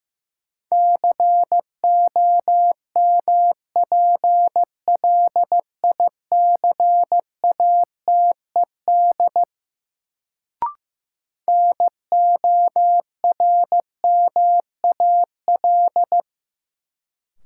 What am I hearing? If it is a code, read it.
Morse code, heard as a keyed tone, about 16 wpm, 700 Hz. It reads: COMPLICATED NORMAL
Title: Sentences from Top 1200 Words - New Word Drill Alt - 15wpm